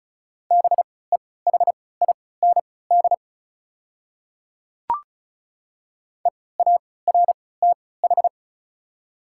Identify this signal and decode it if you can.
Morse code, heard as a keyed tone, about 36 wpm, spaced as for 20 wpm, 690 Hz. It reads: BEHIND EARTH
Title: 3x Character Spacing - Top 500 Words - 35wpm